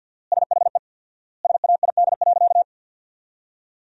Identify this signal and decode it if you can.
Morse code, heard as a keyed tone, about 50 wpm, 700 Hz. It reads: SHE SAID.
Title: 4x Word-Spacing - Sentences from Top 400 Words - 50wpm